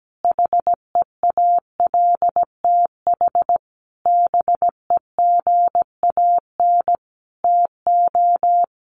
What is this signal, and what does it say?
Morse code, heard as a keyed tone, about 17 wpm, 700 Hz. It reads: HEALTH BEGAN TO
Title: Sentences from Top 1100 Words 17wpm